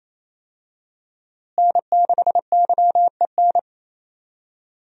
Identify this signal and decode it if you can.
Morse code, heard as a keyed tone, about 28 wpm, 700 Hz. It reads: N6YEN
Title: USA Call Signs - 5 Characters - Repeated 3x - 28wpm